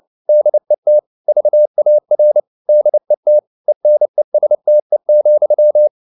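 Morse code, heard as a keyed tone, about 29 wpm, 600 Hz. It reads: DET VAR DET ENESTE,
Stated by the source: Fast (QRQ) Morse at 29 wpm, from CW med Gnister.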